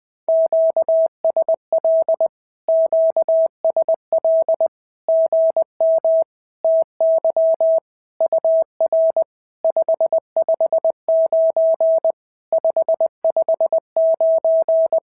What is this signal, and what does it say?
Morse code, heard as a keyed tone, about 20 wpm, 650 Hz. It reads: QSL QSL GM TY UR 559 559
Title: SOTA QSO Practice 20wpm